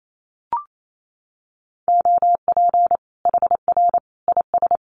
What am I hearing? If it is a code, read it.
Morse code, heard as a keyed tone, about 28 wpm, 700 Hz. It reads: OP HR IS
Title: Common QSO Phrases 28wpm